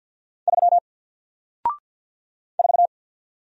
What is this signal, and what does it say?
Morse code, heard as a keyed tone, about 50 wpm, 700 Hz. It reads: 3 4
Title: Single Letter-Number Alt 50wpm